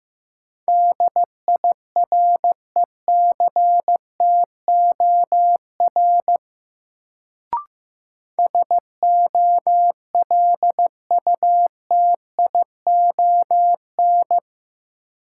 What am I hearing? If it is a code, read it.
Morse code, heard as a keyed tone, about 15 wpm, 700 Hz. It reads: DIRECTOR SOLUTION
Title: Sentences from Top 1300 Words - New Word Drill Alt - 15wpm